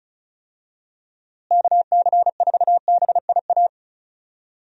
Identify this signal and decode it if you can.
Morse code, heard as a keyed tone, about 34 wpm, 700 Hz. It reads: KC4BIA
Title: USA Call Signs - 6 Characters - Repeated 2x - 35wpm